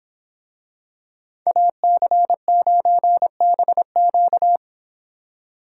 Morse code, heard as a keyed tone, about 25 wpm, 700 Hz. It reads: AC9BQ